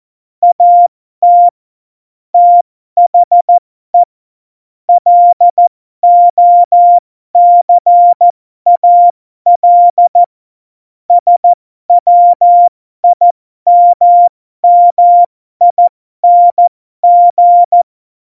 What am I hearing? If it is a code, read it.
Morse code, heard as a keyed tone, about 14 wpm, spaced as for 12 wpm, 700 Hz. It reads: AT THE LOCAL SWIMMING
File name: Morsekurs / Witze